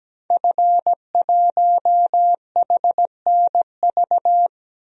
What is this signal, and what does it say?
Morse code, heard as a keyed tone, about 17 wpm, 700 Hz. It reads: F1HNV
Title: DXCC-100 Call Signs - 5 Characters - Repeated 2x - 17wpm